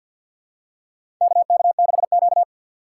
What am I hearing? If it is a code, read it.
Morse code, heard as a keyed tone, about 51 wpm, 700 Hz. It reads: KK6Q
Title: USA Call Signs - 4x Character Spacing - 50wpm